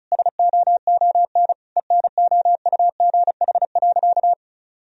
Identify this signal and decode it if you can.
Morse code, heard as a keyed tone, about 36 wpm, 700 Hz. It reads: SOON ENOUGH.
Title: Sentences from Top 1000 Words - Review Part 4 - 35wpm